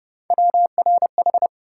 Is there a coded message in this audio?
Yes: Morse code from 0.3 to 1.5 s, heard as a keyed tone, about 29 wpm, 700 Hz. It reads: WRH